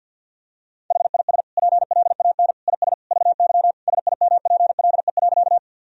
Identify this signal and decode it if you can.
Morse code, heard as a keyed tone, about 50 wpm, 700 Hz. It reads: HIS PLAN IS VY SIMPLE.
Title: Sentences from Top 500 Words and Abbreviations 50wpm